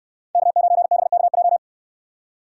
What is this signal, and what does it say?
Morse code, heard as a keyed tone, about 68 wpm, 700 Hz. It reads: K9BKJ